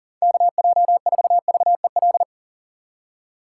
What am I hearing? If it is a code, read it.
Morse code, heard as a keyed tone, about 53 wpm, spaced as for 47 wpm, 700 Hz. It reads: KJ4VEL
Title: Speed Racing - USA 2x3 Call Signs 40wpm